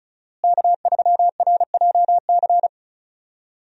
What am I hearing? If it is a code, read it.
Morse code, heard as a keyed tone, about 34 wpm, 700 Hz. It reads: K3RJC
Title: USA Call Signs - 5 Characters - Repeated 2x - 35wpm